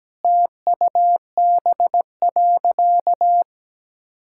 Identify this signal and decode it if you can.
Morse code, heard as a keyed tone, about 17 wpm, 700 Hz. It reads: TUB.
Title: Pangram Challenge v2 - 17wpm